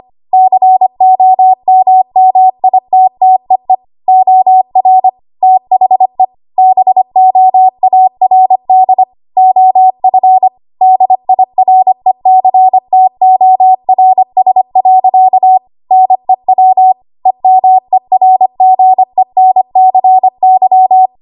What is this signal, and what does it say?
Morse code, heard as a keyed tone, about 25 wpm, 750 Hz. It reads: COMMITTEE OR THE BOARD OF DIRECTORS. NEW EMERGENCY
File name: Archive / 25 WPM / 25WPM